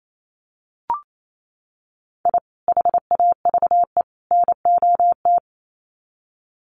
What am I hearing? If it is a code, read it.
Morse code, heard as a keyed tone, about 28 wpm, 700 Hz. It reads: I HAVE NOT
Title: Sets of 3 Words 28wpm